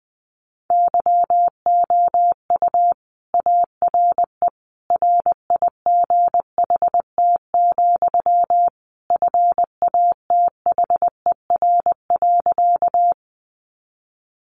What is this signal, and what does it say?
Morse code, heard as a keyed tone, about 20 wpm, 700 Hz. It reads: YOU ARE RIGHT, FATHER.
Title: Sentences from Top 300 Words 20wpm